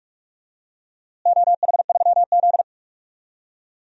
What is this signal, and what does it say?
Morse code, heard as a keyed tone, about 46 wpm, 700 Hz. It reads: OH3Z
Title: DXCC-100 Call Signs - 4 Characters - Repeated 2x - 45wpm